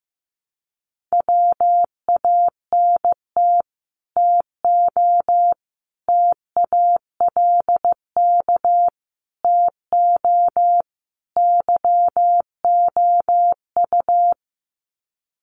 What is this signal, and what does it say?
Morse code, heard as a keyed tone, about 15 wpm, 700 Hz. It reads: WANT TO TALK TO YOU
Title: Sets of 5 Words Encore 15wpm